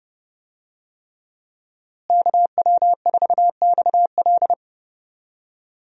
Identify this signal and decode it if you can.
Morse code, heard as a keyed tone, about 30 wpm, 700 Hz. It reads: KW4XL
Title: USA Call Signs - 5 Characters - Repeated 3x - 30wpm